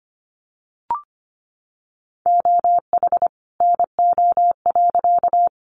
Morse code, heard as a keyed tone, about 25 wpm, 700 Hz. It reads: OH NO.
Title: Additional Easy Phrases 25wpm